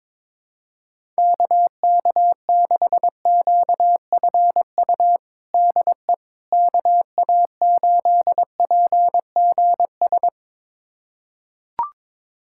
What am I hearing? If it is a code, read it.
Morse code, heard as a keyed tone, about 21 wpm, 700 Hz. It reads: KK6QFU DE KA8PGS
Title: USA Callsign DE Callsign - 6 Characters - 22wpm